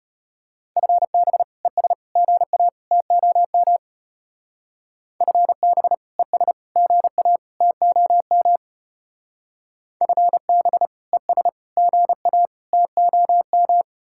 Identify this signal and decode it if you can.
Morse code, heard as a keyed tone, about 33 wpm, 700 Hz. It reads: FB ES GA TOM FB ES GA TOM FB ES GA TOM